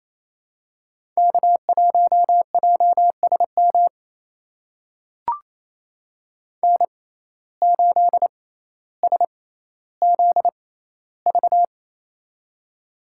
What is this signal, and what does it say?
Morse code, heard as a keyed tone, about 28 wpm, spaced as for 14 wpm, 700 Hz. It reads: K1JSM N8SZV
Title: USA Call Signs - 5 Characters - 6x Character Spacing - 28wpm